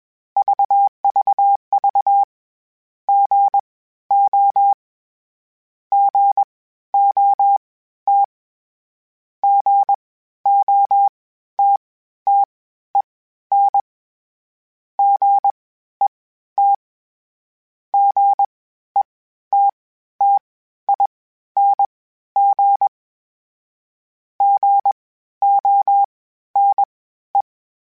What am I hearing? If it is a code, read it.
Morse code, heard as a keyed tone, about 21 wpm, spaced as for 12 wpm, 800 Hz. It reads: VVV GO GOT GOTTEN GET GETTING GONE
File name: Misc / Misc-mp3s / Gotten @20